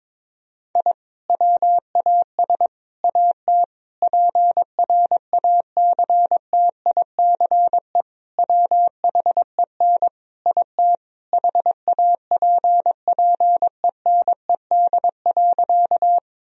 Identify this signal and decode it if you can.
Morse code, heard as a keyed tone, about 22 wpm, 700 Hz. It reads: I WAS AT PRACTICE WHEN IT HAPPENED.
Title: Sentences from Top 1000 Words - Review Part 4 - 22wpm